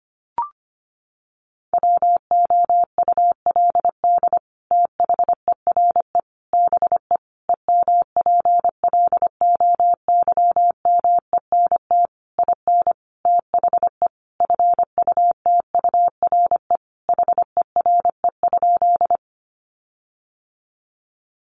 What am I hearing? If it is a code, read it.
Morse code, heard as a keyed tone, about 25 wpm, 700 Hz. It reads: WOULD THERE BE EMPLOYMENT IN THE FUTURE HERE?
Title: Sentences from Top 1500 Words 25wpm